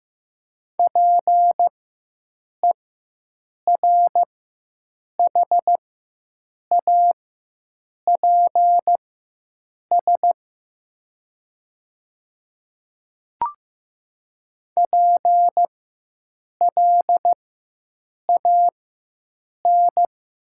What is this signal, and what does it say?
Morse code, heard as a keyed tone, about 15 wpm, spaced as for 7 wpm, 700 Hz. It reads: PERHAPS PLAN
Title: 4x Character Spacing - Top 500 Words - 15wpm